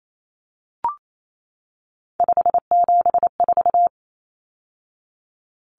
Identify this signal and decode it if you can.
Morse code, heard as a keyed tone, about 28 wpm, 700 Hz. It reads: 574
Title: Numbers - 3-Digit 28wpm